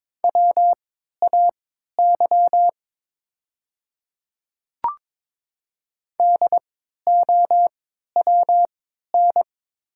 Morse code, heard as a keyed tone, about 22 wpm, spaced as for 12 wpm, 700 Hz. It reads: WAY DOWN
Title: 3x Character Spacing - Top 100 Words - 22wpm